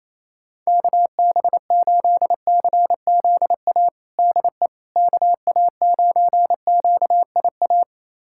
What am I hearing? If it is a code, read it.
Morse code, heard as a keyed tone, about 28 wpm, 700 Hz. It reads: KB8CZA DE KA9QIA
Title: USA Callsign DE Callsign 28wpm